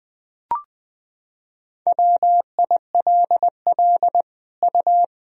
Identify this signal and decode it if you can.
Morse code, heard as a keyed tone, about 21 wpm, 700 Hz. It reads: WILL U
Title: Sentences from Top 200 Words and Abbreviations 20wpm